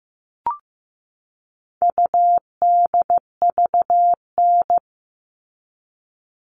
Morse code, heard as a keyed tone, about 15 wpm, 700 Hz. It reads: UDVN
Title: All Cut Numbers - 4-Digit 15wpm